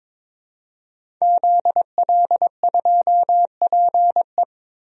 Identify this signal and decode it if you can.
Morse code, heard as a keyed tone, about 22 wpm, 700 Hz. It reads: ZL2PE
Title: DXCC-100 Call Signs - 5 Characters - Repeated 3x - 22wpm